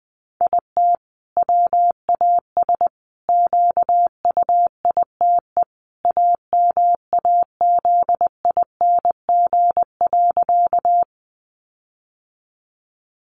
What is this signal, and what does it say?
Morse code, heard as a keyed tone, about 19 wpm, 700 Hz. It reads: IT WAS QUITE AMAZING.
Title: Sentences from Top 1500 Words 20wpm